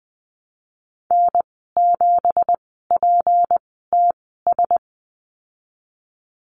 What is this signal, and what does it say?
Morse code, heard as a keyed tone, about 20 wpm, spaced as for 15 wpm, 700 Hz. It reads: N7PTS
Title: 186 Lesson 30g - Letter b - Call Signs 20wpm